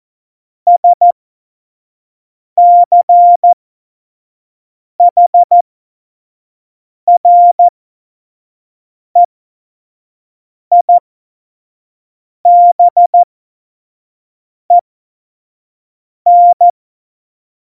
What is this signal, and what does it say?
Morse code, heard as a keyed tone, about 14 wpm, spaced as for 5 wpm, 700 Hz. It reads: SCHREIBEN
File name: Morsekurs / Witze / Witz 30. April 2019, 25 F70